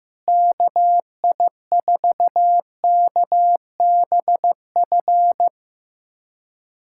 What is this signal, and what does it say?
Morse code, heard as a keyed tone, about 15 wpm, 700 Hz. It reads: KI4KBF
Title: USA Callsign DE Callsign - 6 Characters - 15wpm